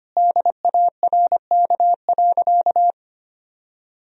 Morse code, heard as a keyed tone, about 25 wpm, 700 Hz. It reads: DARK.
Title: Sentences from Top 500 Words and Abbreviations 25wpm